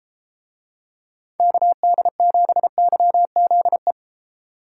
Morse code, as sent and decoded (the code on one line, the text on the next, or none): -.- -.. --... -.-- --.. .
KD7YZE